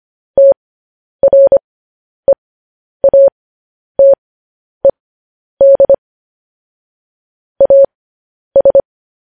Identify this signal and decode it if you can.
Morse code, heard as a keyed tone, about 25 wpm, spaced as for 10 wpm, 550 Hz. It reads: TREATED AS